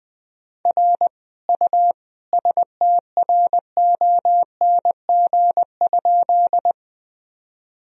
Morse code, heard as a keyed tone, about 20 wpm, 700 Hz. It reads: R U STRONG?
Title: Sentences from Top 500 Words and Abbreviations 20wpm